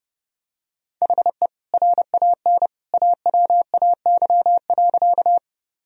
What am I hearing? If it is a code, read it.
Morse code, heard as a keyed tone, about 32 wpm, 700 Hz. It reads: HE RAN AWAY.
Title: Sentences from Top 1000 Words - Review Part 3 - 30wpm